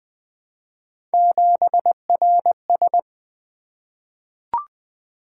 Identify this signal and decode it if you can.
Morse code, heard as a keyed tone, about 20 wpm, 700 Hz. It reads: 7RS
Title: Three Letter-Number Combinations - 20wpm